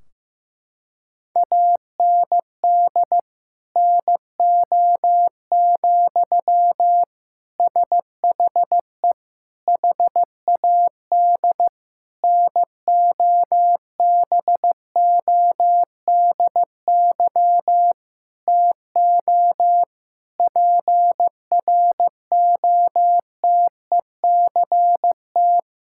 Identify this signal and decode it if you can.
Morse code, heard as a keyed tone, about 15 wpm, 700 Hz. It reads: AND NO, SHE HAD NOBODY TO PROTECT
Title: Sentences from Top 1600 Words 15wpm